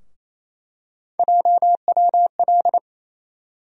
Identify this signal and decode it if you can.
Morse code, heard as a keyed tone, about 28 wpm, 700 Hz. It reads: JWL
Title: USA Call Sign Postfixes - 3 Letters 28wpm